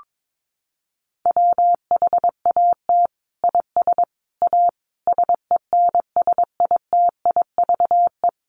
Morse code, heard as a keyed tone, about 22 wpm, 700 Hz. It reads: WHAT IS A SENSITIVE
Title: Sentences from Top 1400 Words 22wpm